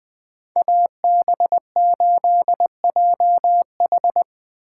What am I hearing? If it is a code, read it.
Morse code, heard as a keyed tone, about 20 wpm, 700 Hz. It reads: AB8JH
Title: Speed Racing - USA 2x2 Call Signs 17wpm